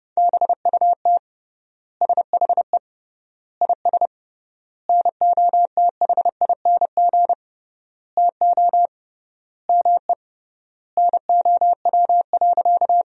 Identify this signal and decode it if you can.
Morse code, heard as a keyed tone, about 30 wpm, 700 Hz. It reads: BUT SHE IS NOTHING TO ME NOW.
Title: 3x Word-Spacing - Sentences from Top 500 Words - 30wpm